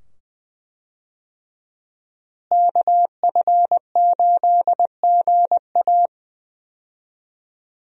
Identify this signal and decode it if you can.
Morse code, heard as a keyed tone, about 20 wpm, 700 Hz. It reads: KF8GA
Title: USA Call Signs - 5 Characters - Repeated 2x - 20wpm